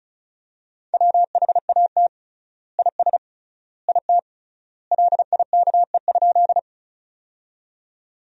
Morse code, heard as a keyed tone, about 36 wpm, 700 Hz. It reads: WHAT IS IT LIKE?
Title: 3x Word-Spacing - Sentences from Top 100 Words - 35wpm